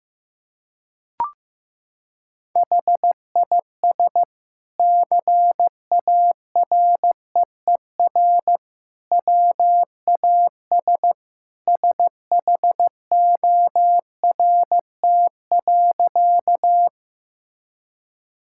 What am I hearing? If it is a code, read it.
Morse code, heard as a keyed tone, about 15 wpm, 700 Hz. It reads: HIS CAREER WAS SHORT.
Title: Sentences from Top 1000 Words - Review Part 6 - 15wpm